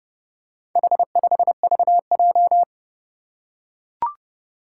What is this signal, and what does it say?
Morse code, heard as a keyed tone, about 30 wpm, 700 Hz. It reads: H5VJ
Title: Four Challenging Letter-Number Combinations 30wpm